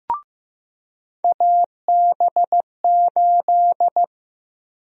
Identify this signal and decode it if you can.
Morse code, heard as a keyed tone, about 16 wpm, 700 Hz. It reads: AB8
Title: USA Call Sign Prefixes - Repeated 2x - 15wpm